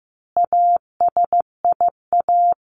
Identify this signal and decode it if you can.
Morse code, heard as a keyed tone, about 15 wpm, 700 Hz. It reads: ASIA
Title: A Sound Of Thunder 15wpm